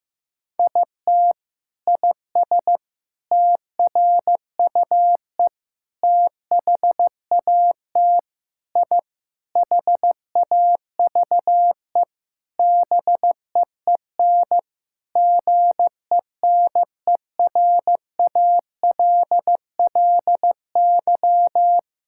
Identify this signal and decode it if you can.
Morse code, heard as a keyed tone, about 15 wpm, 700 Hz. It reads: IT IS TRUE THAT I HAVE BEEN GENERALLY
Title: Sentences from Top 1000 Words - Review Part 6 - 15wpm